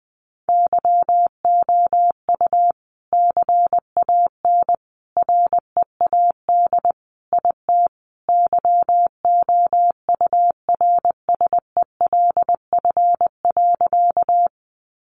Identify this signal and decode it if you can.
Morse code, heard as a keyed tone, about 19 wpm, 700 Hz. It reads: YOU CAN READ IT YOURSELF.
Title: Sentences from Top 600 Words 20wpm